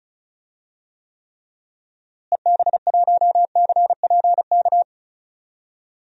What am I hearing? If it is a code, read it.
Morse code, heard as a keyed tone, about 34 wpm, 700 Hz. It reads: EB1CPK
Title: DXCC-100 Call Signs - 6 Characters - Repeated 3x - 35wpm